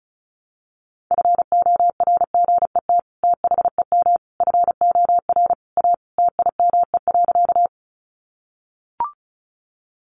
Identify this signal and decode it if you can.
Morse code, heard as a keyed tone, about 34 wpm, 700 Hz. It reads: FORGET THEM FOR A TIME.
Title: Sentences from Top 1200 Words 35wpm